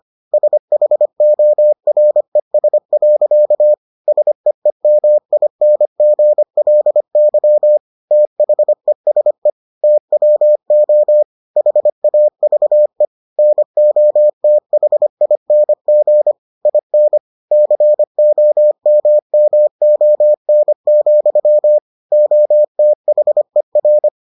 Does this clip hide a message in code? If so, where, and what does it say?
Yes: Morse code from 0.3 to 24.1 s, heard as a keyed tone, about 25 wpm, 600 Hz. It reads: SHORES. SEEMINGLY THESE TWO HAVE NOTHING IN COMMON, OTHER